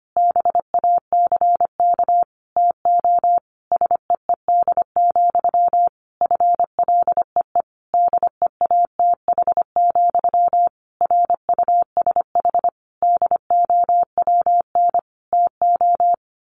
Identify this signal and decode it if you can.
Morse code, heard as a keyed tone, about 25 wpm, 700 Hz. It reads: BACK TO SEED, FLEE DEATH, RUSH DOWN TO
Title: A Sound Of Thunder 25wpm